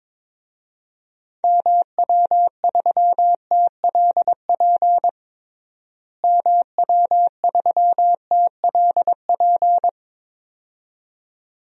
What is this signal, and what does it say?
Morse code, heard as a keyed tone, about 22 wpm, 700 Hz. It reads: MW3TLP MW3TLP
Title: DXCC-100 Call Signs - 6 Characters - Repeated 2x - 22wpm